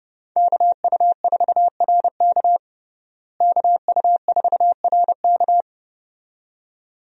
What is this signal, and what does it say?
Morse code, heard as a keyed tone, about 30 wpm, 700 Hz. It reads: KU4RK KU4RK
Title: USA Call Signs - 5 Characters - Repeated 2x - 30wpm